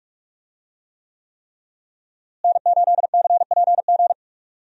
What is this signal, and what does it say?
Morse code, heard as a keyed tone, about 46 wpm, 700 Hz. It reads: N8CPG
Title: USA Call Signs - 5 Characters - Repeated 3x - 45wpm